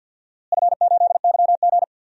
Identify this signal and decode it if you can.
Morse code, heard as a keyed tone, about 53 wpm, 700 Hz. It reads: F8YG
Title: DXCC-100 Call Signs - 4 Characters - Repeated 2x - 50wpm